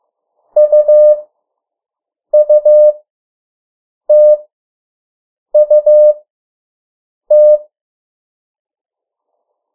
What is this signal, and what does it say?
Morse code, heard as a keyed tone, about 15 wpm, spaced as for 6 wpm, 600 Hz. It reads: UUTUT
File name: morse mp3 / 03 urok 2